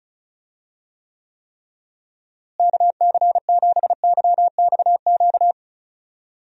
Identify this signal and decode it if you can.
Morse code, heard as a keyed tone, about 36 wpm, 700 Hz. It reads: KC7YXQ
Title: USA Call Signs - 6 Characters - Repeated 3x - 35wpm